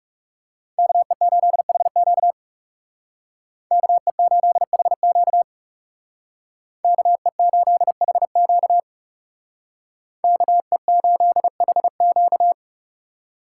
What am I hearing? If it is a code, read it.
Morse code, heard as a keyed tone, about 38 wpm, 700 Hz. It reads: KE8HQ KE8HQ KE8HQ KE8HQ